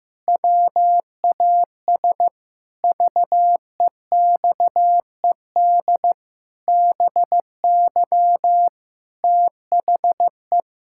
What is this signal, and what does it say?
Morse code, heard as a keyed tone, about 15 wpm, 700 Hz. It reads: WAS VEXED BY THE
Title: Pangram Challenge v2 - 15wpm